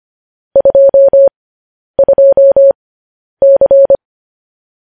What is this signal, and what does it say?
Morse code, heard as a keyed tone, about 25 wpm, spaced as for 10 wpm, 550 Hz. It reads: 22C
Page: news headlines in morse code at 10 wpm